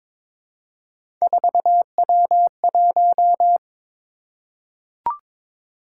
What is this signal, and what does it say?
Morse code, heard as a keyed tone, about 22 wpm, 700 Hz. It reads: 4W1
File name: practice / 22wpm / Three Letter-Number Combinations - 22wpm